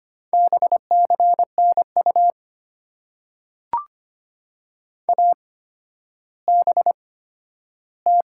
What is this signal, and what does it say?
Morse code, heard as a keyed tone, about 25 wpm, spaced as for 12 wpm, 700 Hz. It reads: BCNU ABT
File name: Top 90 QSO Elements - 8x Character Spacing - 25wpm